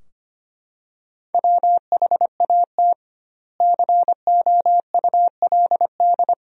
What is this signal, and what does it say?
Morse code, heard as a keyed tone, about 25 wpm, 700 Hz. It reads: WHAT COULD
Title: 2x Word-Spacing - Sentences from Top 200 Words - 25wpm